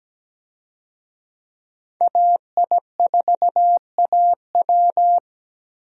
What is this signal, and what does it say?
Morse code, heard as a keyed tone, about 17 wpm, 700 Hz. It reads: AI4AW
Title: USA Call Signs - 5 Characters - Repeated 3x - 17wpm